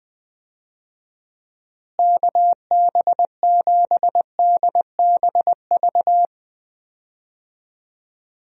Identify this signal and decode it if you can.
Morse code, heard as a keyed tone, about 20 wpm, 700 Hz. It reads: KB7DBV
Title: USA Call Signs - 6 Characters - Repeated 2x - 20wpm